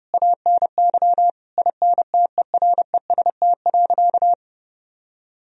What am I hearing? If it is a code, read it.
Morse code, heard as a keyed tone, about 32 wpm, 700 Hz. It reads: ANY INTEREST.
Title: Sentences from Top 1000 Words - Review Part 2 - 30wpm